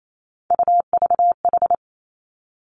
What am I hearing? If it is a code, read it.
Morse code, heard as a keyed tone, about 28 wpm, 700 Hz. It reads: UVH
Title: Three Challenging Letter-Number Combinations 28wpm